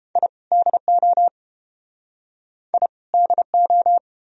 Morse code, heard as a keyed tone, about 31 wpm, 700 Hz. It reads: I DO I DO